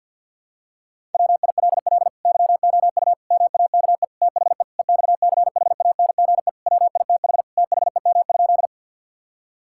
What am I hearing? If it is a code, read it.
Morse code, heard as a keyed tone, about 50 wpm, 700 Hz. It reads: WILL YOU MAKE THE EXCHANGE WITH THEM?
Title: Sentences from Top 1000 Words - Review Part 6 - 50wpm